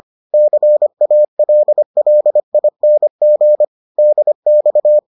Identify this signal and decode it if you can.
Morse code, heard as a keyed tone, about 25 wpm, 600 Hz. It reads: CALLING DX